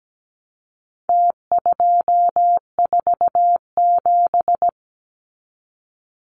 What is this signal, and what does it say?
Morse code, heard as a keyed tone, about 17 wpm, 700 Hz. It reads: T247
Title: Numbers - Contest Style - 4-Digit 17wpm